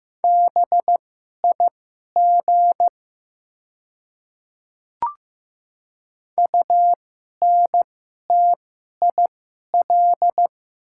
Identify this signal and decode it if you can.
Morse code, heard as a keyed tone, about 15 wpm, spaced as for 11 wpm, 700 Hz. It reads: BIG UNTIL
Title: 2x Character Spacing - Top 500 Words - 15wpm